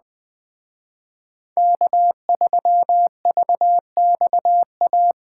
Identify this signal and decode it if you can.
Morse code, heard as a keyed tone, about 20 wpm, 700 Hz. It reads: K3VXA
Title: Speed Racing - USA 1x3 Call Signs 17wpm